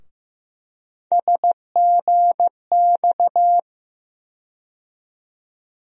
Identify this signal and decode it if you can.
Morse code, heard as a keyed tone, about 15 wpm, 700 Hz. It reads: SGX
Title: USA Call Sign Postfixes - 3 Letters 15wpm